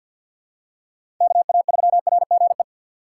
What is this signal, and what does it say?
Morse code, heard as a keyed tone, about 51 wpm, 700 Hz. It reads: KA3RGE
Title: USA Call Signs - 4x Character Spacing - 50wpm